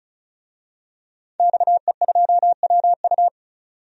Morse code, heard as a keyed tone, about 34 wpm, 700 Hz. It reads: XE2WU
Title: DXCC-100 Call Signs - 5 Characters - 35wpm